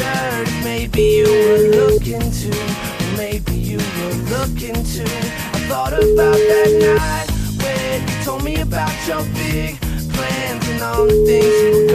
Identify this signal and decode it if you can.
Ringback tone (Europe, 425 Hz), with noise.